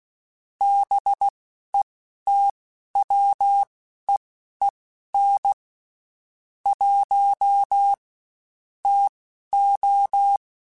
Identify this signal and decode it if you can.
Morse code, heard as a keyed tone, about 16 wpm, spaced as for 11 wpm, 800 Hz. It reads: BETWEEN 1 TO
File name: Spaced Characters / Spaced Characters-mp3s / Weather 5 5 sp @15